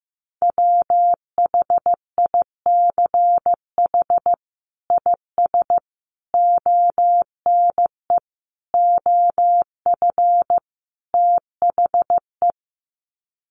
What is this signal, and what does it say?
Morse code, heard as a keyed tone, about 15 wpm, 700 Hz. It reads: WHICH IS ONE OF THE
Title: Sets of 5 Words Encore 15wpm